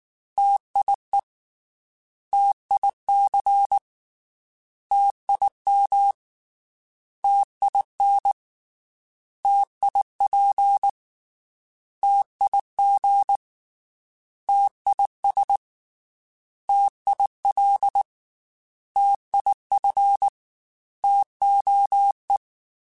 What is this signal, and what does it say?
Morse code, heard as a keyed tone, about 19 wpm, 800 Hz. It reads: TIE TIC TIM TIN TIP TIG TIS TIL TIF TOE
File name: T Prefixes / T mp3s